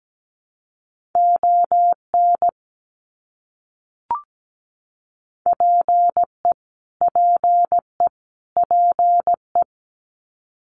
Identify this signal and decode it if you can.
Morse code, heard as a keyed tone, about 17 wpm, 700 Hz. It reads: ON PE PE PE
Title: Canadian Provinces and Territories - Repeated 3x - 17wpm